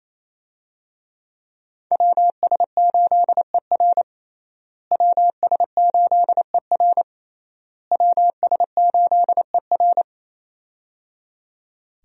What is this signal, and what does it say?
Morse code, heard as a keyed tone, about 28 wpm, 700 Hz. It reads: WS8ER WS8ER WS8ER
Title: USA Call Signs - 5 Characters - Repeated 3x - 28wpm